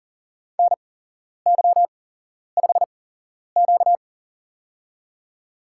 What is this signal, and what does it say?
Morse code, heard as a keyed tone, about 39 wpm, spaced as for 11 wpm, 700 Hz. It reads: NY5Q
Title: USA Call Signs - 4 Characters - 8x Character Spacing - 40wpm